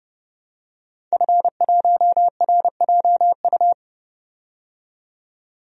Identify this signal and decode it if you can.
Morse code, heard as a keyed tone, about 31 wpm, 700 Hz. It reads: F1RJU
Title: DXCC-100 Call Signs - 5 Characters - 30wpm